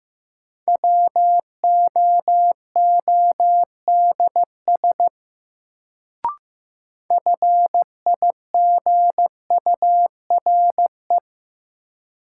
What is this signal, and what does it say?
Morse code, heard as a keyed tone, about 16 wpm, 700 Hz. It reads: WOODS FIGURE
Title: Sentences from Top 1000 Words - Review Part 2 - New Word Drill - 15wpm